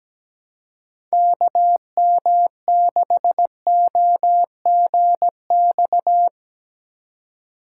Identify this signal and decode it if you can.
Morse code, heard as a keyed tone, about 17 wpm, 700 Hz. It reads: KM6OGX
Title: USA Call Signs - 6 Characters - 4x Character Spacing - 17wpm